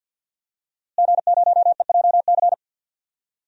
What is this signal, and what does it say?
Morse code, heard as a keyed tone, about 50 wpm, 700 Hz. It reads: G0EJC
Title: DXCC-100 Call Signs - 5 Characters - Repeated 3x - 50wpm